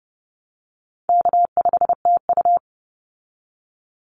Morse code, heard as a keyed tone, about 30 wpm, 700 Hz. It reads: K5TU
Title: Easy Call Signs Repeated 2x - 30wpm